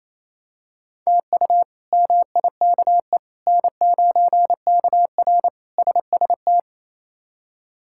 Morse code, heard as a keyed tone, about 28 wpm, 700 Hz. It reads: TU MIKE N9KR SST
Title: K1USN SST QSO Practice - 28wpm